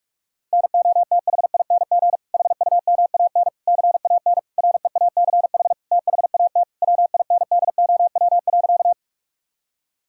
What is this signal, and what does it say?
Morse code, heard as a keyed tone, about 46 wpm, 700 Hz. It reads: NOTHING HUMAN CAN REACH THAT WINDOW.
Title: Sentences from Top 1600 Words 45wpm